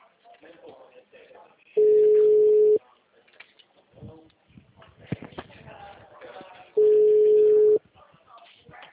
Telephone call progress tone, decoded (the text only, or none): ringback tone